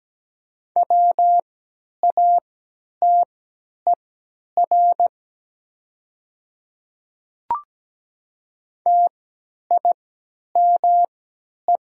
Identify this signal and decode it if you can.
Morse code, heard as a keyed tone, about 17 wpm, spaced as for 10 wpm, 700 Hz. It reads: WATER TIME
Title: 3x Character Spacing - Top 100 Words - 17wpm